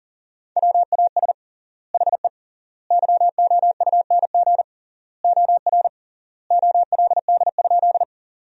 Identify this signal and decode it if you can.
Morse code, heard as a keyed tone, about 40 wpm, 700 Hz. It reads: WAS HE YOUNG OR OLD?
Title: 3x Word-Spacing - Sentences from Top 400 Words - 40wpm